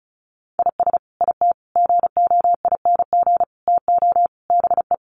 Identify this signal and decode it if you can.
Morse code, heard as a keyed tone, about 34 wpm, 700 Hz. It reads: IS IT GOING TO BE
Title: Sets of 5 Words Encore2 35wpm